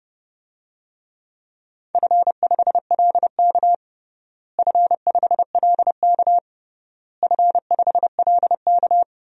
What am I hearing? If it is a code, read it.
Morse code, heard as a keyed tone, about 29 wpm, 700 Hz. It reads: F5LK F5LK F5LK
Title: DXCC-100 Call Signs - 4 Characters - Repeated 3x - 30wpm